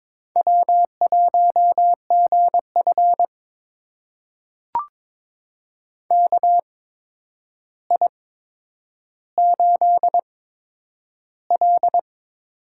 Morse code, heard as a keyed tone, about 22 wpm, spaced as for 9 wpm, 700 Hz. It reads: W1GF KI8L
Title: USA Call Signs - 4 Characters - 8x Character Spacing - 22wpm